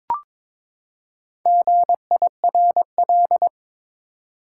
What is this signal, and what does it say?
Morse code, heard as a keyed tone, about 22 wpm, 700 Hz. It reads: GIRL
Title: Four Letter Words - 22wpm